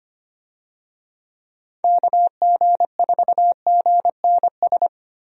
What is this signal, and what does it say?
Morse code, heard as a keyed tone, about 25 wpm, 700 Hz. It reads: KG4GNS
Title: USA Call Signs - 6 Characters - Repeated 2x - 25wpm